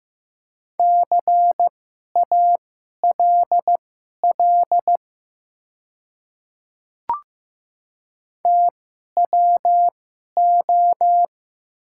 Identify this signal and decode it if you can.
Morse code, heard as a keyed tone, about 15 wpm, spaced as for 11 wpm, 690 Hz. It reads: CALL TWO
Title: 2x Character Spacing - Top 100 Words - 15wpm